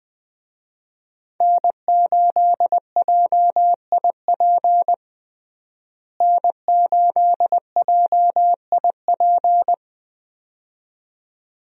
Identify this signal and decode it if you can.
Morse code, heard as a keyed tone, about 20 wpm, 700 Hz. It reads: N8JIP N8JIP
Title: USA Call Signs - 5 Characters - Repeated 2x - 20wpm